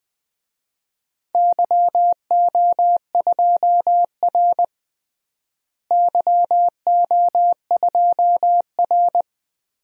Morse code, heard as a keyed tone, about 20 wpm, 700 Hz. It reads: YO2R YO2R
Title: DXCC-100 Call Signs - 4 Characters - Repeated 3x - 20wpm